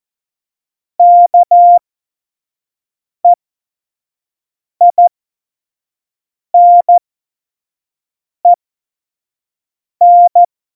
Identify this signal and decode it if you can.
Morse code, heard as a keyed tone, about 14 wpm, spaced as for 5 wpm, 700 Hz. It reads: KEINEN
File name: Morsekurs / Witze / Witz 10.Juli 2018, 25 F70